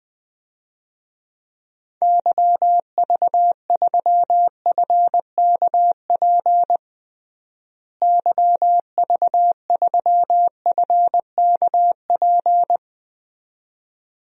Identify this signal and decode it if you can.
Morse code, heard as a keyed tone, about 20 wpm, 700 Hz. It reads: YV3FKP YV3FKP
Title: DXCC-100 Call Signs - 6 Characters - Repeated 2x - 20wpm